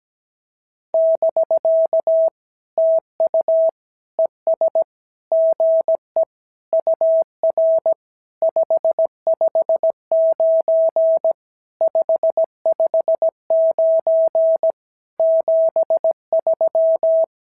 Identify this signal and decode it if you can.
Morse code, heard as a keyed tone, about 17 wpm, 650 Hz. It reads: <BK> TU ES GE UR 559 559 73
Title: SOTA QSO Practice 17wpm